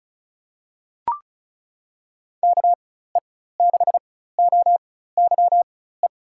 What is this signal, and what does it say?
Morse code, heard as a keyed tone, about 36 wpm, spaced as for 16 wpm, 700 Hz. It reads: KE6OYE
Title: USA Call Signs - 6 Characters - 4x Character Spacing - 35wpm